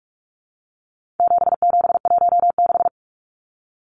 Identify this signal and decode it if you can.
Morse code, heard as a keyed tone, about 44 wpm, 700 Hz. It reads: 7716
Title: Numbers - 4-Digit 45wpm